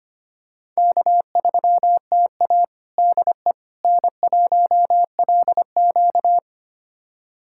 Morse code, heard as a keyed tone, about 25 wpm, 700 Hz. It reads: K3TA DE N1LQ
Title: USA Callsign DE Callsign - 4 Characters - 25wpm